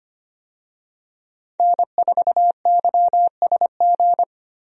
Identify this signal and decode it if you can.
Morse code, heard as a keyed tone, about 25 wpm, 700 Hz. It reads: N4YSG
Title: USA Call Signs - 5 Characters - Repeated 3x - 25wpm